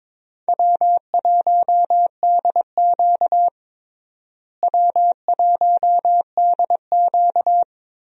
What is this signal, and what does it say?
Morse code, heard as a keyed tone, about 21 wpm, 700 Hz. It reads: W1DQ W1DQ
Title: USA Call Signs - 4 Characters - Repeated 2x - 22wpm